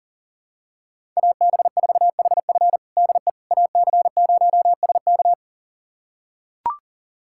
Morse code, heard as a keyed tone, about 37 wpm, 700 Hz. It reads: AB4HF DE AC0SK
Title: USA Callsign DE Callsign - 5 Characters - 40wpm